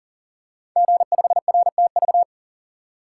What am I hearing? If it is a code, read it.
Morse code, heard as a keyed tone, about 42 wpm, 700 Hz. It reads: G5RTV
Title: Code Groups - 5 Letter:Numbers - 40wpm